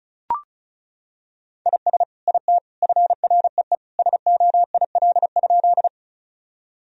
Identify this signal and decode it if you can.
Morse code, heard as a keyed tone, about 36 wpm, 700 Hz. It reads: IS IT FREE SOIL?
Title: Sentences from Top 1000 Words - Review Part 4 - 35wpm